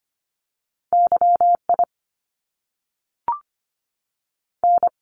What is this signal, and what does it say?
Morse code, heard as a keyed tone, about 25 wpm, 700 Hz. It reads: YI N
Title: Top Two Letter Combinations in Words 25wpm